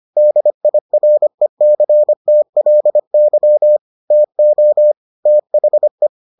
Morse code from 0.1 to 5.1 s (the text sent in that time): DIRECTLY TO